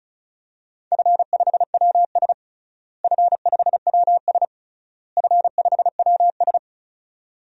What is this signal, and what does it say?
Morse code, heard as a keyed tone, about 36 wpm, 700 Hz. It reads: F5WS F5WS F5WS
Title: DXCC-100 Call Signs - 4 Characters - Repeated 3x - 35wpm